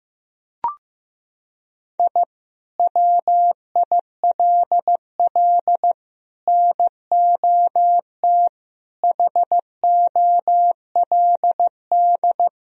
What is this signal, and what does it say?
Morse code, heard as a keyed tone, about 15 wpm, 700 Hz. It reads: I WILL NOT HOLD
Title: Sentences from Top 1000 Words - Review Part 2 - 15wpm